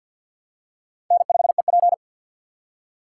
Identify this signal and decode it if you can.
Morse code, heard as a keyed tone, about 47 wpm, 700 Hz. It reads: N5EP
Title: USA Call Signs - 4 Characters - 3x Character Spacing - 50wpm